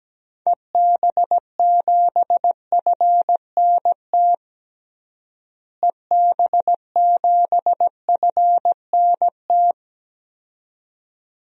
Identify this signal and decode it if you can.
Morse code, heard as a keyed tone, about 17 wpm, 700 Hz. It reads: EB7FNT EB7FNT
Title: DXCC-100 Call Signs - 6 Characters - Repeated 2x - 17wpm